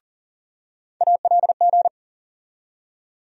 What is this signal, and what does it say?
Morse code, heard as a keyed tone, about 40 wpm, 700 Hz. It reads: ALG
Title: USA Call Sign Postfixes - 3 Letters 40wpm